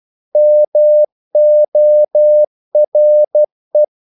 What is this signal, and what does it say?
Morse code, heard as a keyed tone, about 12 wpm, 600 Hz. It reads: MORE
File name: CW / Practice / EnglishQuotations0112